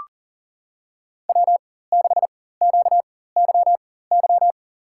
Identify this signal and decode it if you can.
Morse code, heard as a keyed tone, about 39 wpm, spaced as for 18 wpm, 700 Hz. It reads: W6QYY
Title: USA Call Signs - 4x Character Spacing - 40wpm